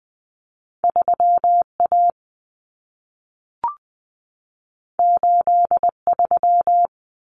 Morse code, heard as a keyed tone, about 20 wpm, 700 Hz. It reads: 3A 83
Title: Common Cut Numbers - 2-Digit 20wpm